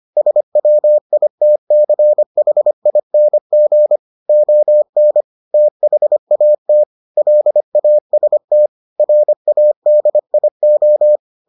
Morse code, heard as a keyed tone, about 25 wpm, 600 Hz. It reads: SWITCHING ON THAT LAST RADIO